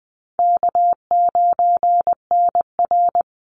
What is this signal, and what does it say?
Morse code, heard as a keyed tone, about 20 wpm, 700 Hz. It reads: K9NR
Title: Easy Call Signs - 20wpm